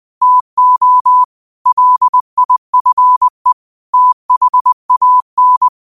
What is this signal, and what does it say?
Morse code, heard as a keyed tone, about 20 wpm, 1000 Hz. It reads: TO LIFE THAN
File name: qotdfiles / 20wpm